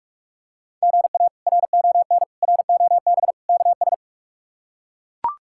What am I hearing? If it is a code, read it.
Morse code, heard as a keyed tone, about 42 wpm, 700 Hz. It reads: GA RON ROB KS
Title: K1USN SST QSO Practice - 45wpm